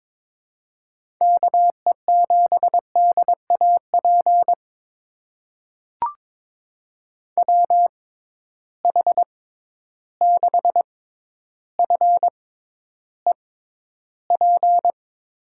Morse code, heard as a keyed tone, about 22 wpm, spaced as for 11 wpm, 700 Hz. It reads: KE7DAP WH6FEP
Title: USA Call Signs - 6 Characters - 6x Character Spacing - 22wpm